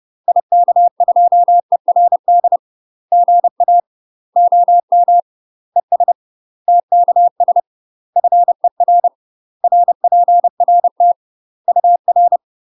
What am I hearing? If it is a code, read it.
Morse code, heard as a keyed tone, about 30 wpm, 700 Hz. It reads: IK2ERD GA OM ES TKS FER RPRT UR
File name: UK MASTER QSOs TRAINING / 30WPM / UK MASTER QSOs 0005